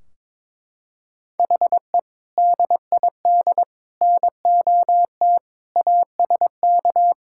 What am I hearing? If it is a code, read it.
Morse code, heard as a keyed tone, about 22 wpm, 700 Hz. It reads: HE DID NOT ASK
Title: Sentences from Top 1600 Words 22wpm